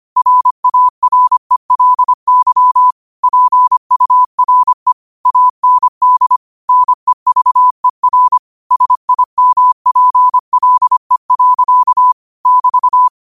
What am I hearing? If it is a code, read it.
Morse code, heard as a keyed tone, about 25 wpm, 1000 Hz. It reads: RARELY PURE AND NEVER SIMPLE. <BT>
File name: qotdfiles / 25wpm